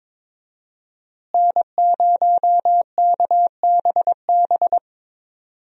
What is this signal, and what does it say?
Morse code, heard as a keyed tone, about 22 wpm, 700 Hz. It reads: N0KBB
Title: USA Call Signs - 5 Characters - Repeated 2x - 22wpm